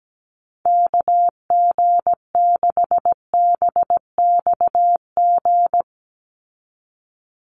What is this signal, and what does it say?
Morse code, heard as a keyed tone, about 17 wpm, 700 Hz. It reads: KG6BXG
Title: USA Call Signs 17wpm